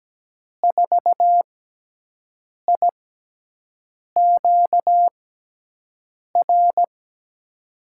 Morse code, heard as a keyed tone, about 17 wpm, spaced as for 6 wpm, 700 Hz. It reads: 4IQR
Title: USA Call Signs - 5 Characters - 6x Character Spacing - 17wpm